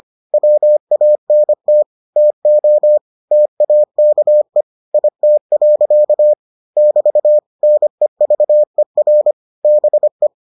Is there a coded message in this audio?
Yes: Morse code from 0.3 to 10.3 s, heard as a keyed tone, about 25 wpm, 600 Hz. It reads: WANT TO TAKE IT. <BT> NEVER BE